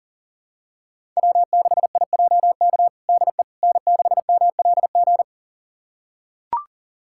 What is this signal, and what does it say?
Morse code, heard as a keyed tone, about 40 wpm, 700 Hz. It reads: W6IJK DE N6MLG
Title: USA Callsign DE Callsign - 5 Characters - 40wpm